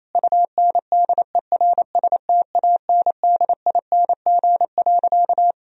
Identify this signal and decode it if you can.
Morse code, heard as a keyed tone, about 27 wpm, 690 Hz. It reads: UNDERSTANDING.
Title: Sentences from Top 1000 Words - Review Part 3 - 28wpm